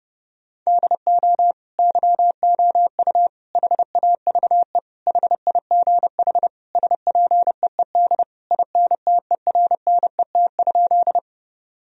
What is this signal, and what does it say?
Morse code, heard as a keyed tone, about 32 wpm, 700 Hz. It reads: DO YOU HAVE HIGH SPEED INTERNET?
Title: Sentences from Top 1000 Words - Review Part 4 - 30wpm